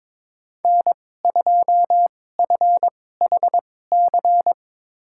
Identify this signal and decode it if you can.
Morse code, heard as a keyed tone, about 22 wpm, spaced as for 16 wpm, 700 Hz. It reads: N2FHC
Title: USA Call Signs - 5 Characters - 2x Character Spacing - 22wpm